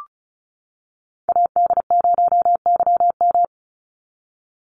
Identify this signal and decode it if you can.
Morse code, heard as a keyed tone, about 36 wpm, 700 Hz. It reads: AD0YM
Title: NA SOTA Chaser Callsigns 35wpm